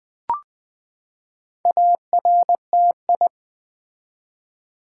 Morse code, heard as a keyed tone, about 20 wpm, 700 Hz. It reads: ARTI
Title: Build A Word - 4 Letters At A Time - 10-Letter Words - 20wpm